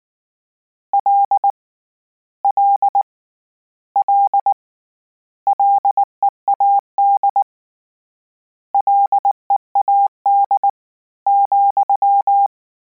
Morse code, heard as a keyed tone, about 19 wpm, 800 Hz. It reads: L L L LEAD LEAD ,
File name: Combinations / Combinations-mp3s / Combinations@18